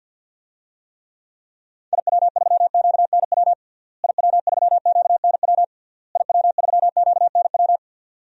Morse code, heard as a keyed tone, about 52 wpm, 700 Hz. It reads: IW3QNW IW3QNW IW3QNW
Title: DXCC-100 Call Signs - 6 Characters - Repeated 3x - 50wpm